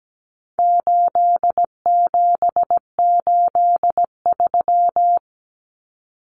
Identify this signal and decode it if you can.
Morse code, heard as a keyed tone, about 17 wpm, 700 Hz. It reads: 8783
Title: Numbers - 4-Digit 17wpm